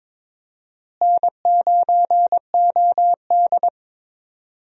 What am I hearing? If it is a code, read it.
Morse code, heard as a keyed tone, about 22 wpm, 700 Hz. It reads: N9OD
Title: USA Call Signs - 4 Characters - Repeated 2x - 22wpm